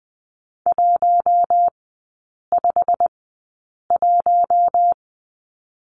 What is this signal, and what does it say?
Morse code, heard as a keyed tone, about 20 wpm, spaced as for 8 wpm, 690 Hz. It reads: 151
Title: 191 Lesson 31e - Letter v - Three Characters 20wpm